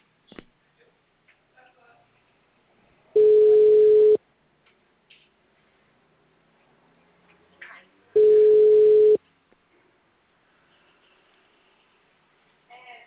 Ringback tone (Europe, 425 Hz).
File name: out- -2012- - - .52